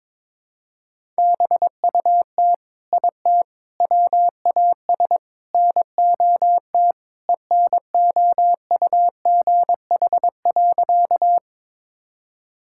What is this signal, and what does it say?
Morse code, heard as a keyed tone, about 22 wpm, 700 Hz. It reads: BUT IT WAS NOT ENOUGH.
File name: practice / 22wpm / Sentences with 4-Word Phrases - 22wpm